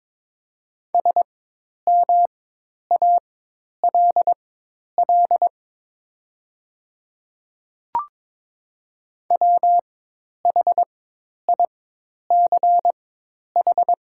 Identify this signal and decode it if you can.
Morse code, heard as a keyed tone, about 22 wpm, spaced as for 10 wpm, 700 Hz. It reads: SMALL WHICH
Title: 4x Character Spacing - Top 300 Words - 22wpm